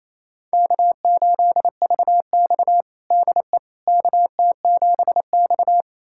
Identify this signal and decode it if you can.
Morse code, heard as a keyed tone, about 28 wpm, 700 Hz. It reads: K8VX DE KT7X
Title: USA Callsign DE Callsign - 4 Characters - 28wpm